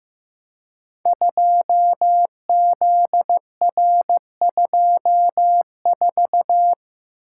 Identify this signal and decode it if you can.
Morse code, heard as a keyed tone, about 15 wpm, 700 Hz. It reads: 2ZR24
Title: Code Groups - 5 Letter:Numbers - 15wpm